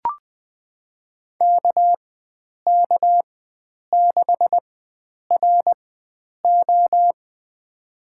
Morse code, heard as a keyed tone, about 20 wpm, spaced as for 9 wpm, 700 Hz. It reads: KK6RO
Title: USA Call Signs - 5 Characters - 4x Character Spacing - 20wpm